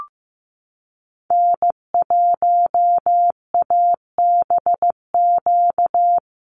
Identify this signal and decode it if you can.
Morse code, heard as a keyed tone, about 15 wpm, 700 Hz. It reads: N1ABQ
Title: USA Call Signs Repeated 3x 15wpm